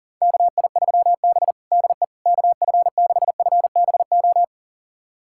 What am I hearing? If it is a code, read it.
Morse code, heard as a keyed tone, about 37 wpm, 700 Hz. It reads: KI3B DE KF6FBO
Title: USA Callsign DE Callsign 40wpm